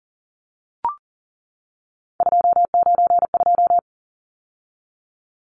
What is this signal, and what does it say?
Morse code, heard as a keyed tone, about 40 wpm, 700 Hz. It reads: 292
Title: Numbers - 3-Digit 40wpm